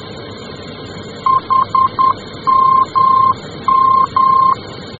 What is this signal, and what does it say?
Morse code, heard as a keyed tone, about 10 wpm, 1050 Hz. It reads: HMM
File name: germany / mx / HMM-D